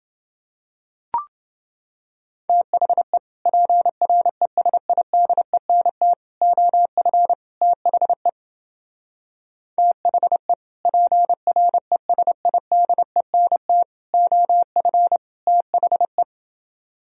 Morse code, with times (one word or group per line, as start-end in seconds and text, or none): 2.5-3.2 THE
3.5-6.1 PRESIDENT
6.4-7.3 OF
7.6-8.3 THE
9.8-10.5 THE
10.9-13.8 PRESIDENT
14.1-15.2 OF
15.5-16.2 THE